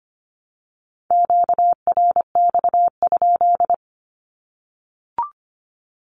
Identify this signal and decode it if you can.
Morse code, heard as a keyed tone, about 25 wpm, 700 Hz. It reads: QRX?
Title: Q-Codes 25wpm